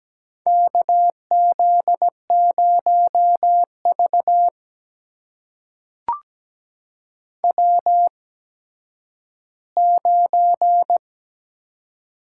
Morse code, heard as a keyed tone, about 17 wpm, spaced as for 10 wpm, 700 Hz. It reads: KZ0V W9
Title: USA Call Signs - 8x Character Spacing - 17wpm